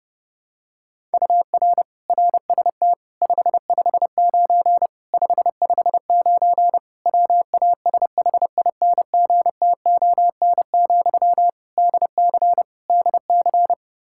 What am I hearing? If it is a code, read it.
Morse code, heard as a keyed tone, about 32 wpm, 700 Hz. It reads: UR RST 559 559 WASHINGTON, DC DC
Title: Practice QSO - Exchange 1 - Beginning - 30wpm